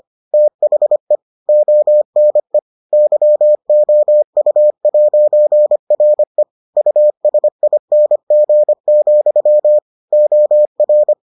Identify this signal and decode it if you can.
Morse code, heard as a keyed tone, about 25 wpm, 600 Hz. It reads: THE ONE YOU'RE USING, OR